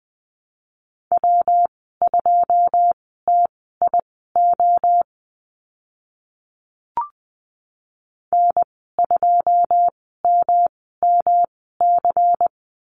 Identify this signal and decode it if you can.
Morse code, heard as a keyed tone, about 20 wpm, spaced as for 15 wpm, 700 Hz. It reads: W2TIO N2MMC
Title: 122 Lesson 20g - Number 3 - Call Signs 20wpm